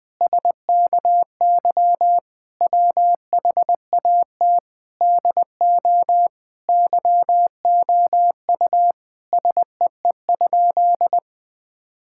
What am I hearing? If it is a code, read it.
Morse code, heard as a keyed tone, about 20 wpm, 700 Hz. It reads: SKY WHAT DO YOU SEE?
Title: Sentences with 3-Word Phrases - 20wpm